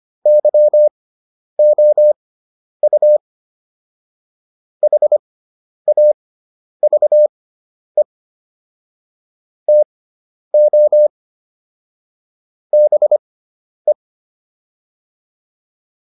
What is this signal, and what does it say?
Morse code, heard as a keyed tone, about 25 wpm, spaced as for 10 wpm, 600 Hz. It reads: YOU HAVE TO BE
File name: WD304 10